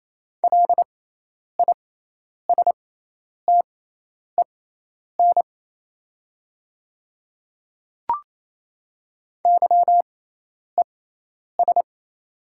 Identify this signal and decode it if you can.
Morse code, heard as a keyed tone, about 28 wpm, spaced as for 10 wpm, 690 Hz. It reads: LISTEN YES
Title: 6x Character Spacing - Top 500 Words - 28wpm